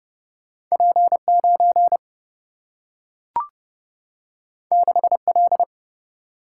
Morse code, heard as a keyed tone, about 30 wpm, 700 Hz. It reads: P9 6L
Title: Single-Letter Single-Number 30wpm